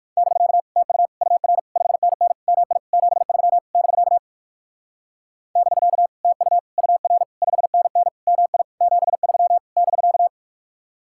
Morse code, heard as a keyed tone, about 56 wpm, 700 Hz. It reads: <BK> TU UR 5NN MI 73 <BK> <BK> TU UR 5NN MI 73 <BK>
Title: Speed-Racing Common QSO Phrases Encore - 45wpm